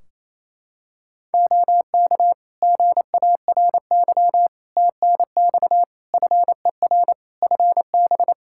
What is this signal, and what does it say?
Morse code, heard as a keyed tone, about 28 wpm, 700 Hz. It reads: OK GARY TNX FER FB
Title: Practice QSO - Exchange 4 - Ending - 28wpm